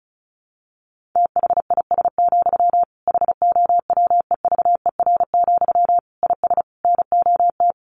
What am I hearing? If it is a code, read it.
Morse code, heard as a keyed tone, about 34 wpm, 700 Hz. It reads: THIS, HOWEVER, IS NOT
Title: Sentences from Top 1200 Words 35wpm